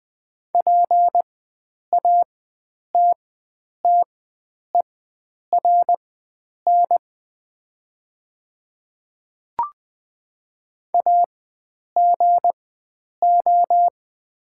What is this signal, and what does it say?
Morse code, heard as a keyed tone, about 20 wpm, spaced as for 9 wpm, 700 Hz. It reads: PATTERN AGO
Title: 4x Character Spacing - Top 500 Words - 20wpm